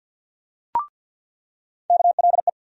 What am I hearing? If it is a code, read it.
Morse code, heard as a keyed tone, about 48 wpm, 700 Hz. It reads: KLE
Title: USA Call Sign Postfixes - 3 Letters 50wpm